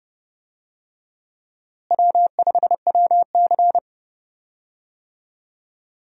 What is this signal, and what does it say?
Morse code, heard as a keyed tone, about 30 wpm, 700 Hz. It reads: W5WC
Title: USA Call Signs - 4 Characters - Repeated 3x - 30wpm